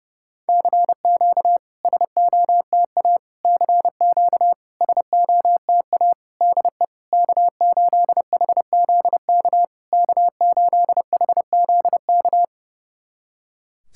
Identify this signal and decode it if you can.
Morse code, heard as a keyed tone, about 30 wpm, 700 Hz. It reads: CQ SOTA CQ SOTA DE K8HZK K8HZK
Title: SOTA QSO Practice 30wpm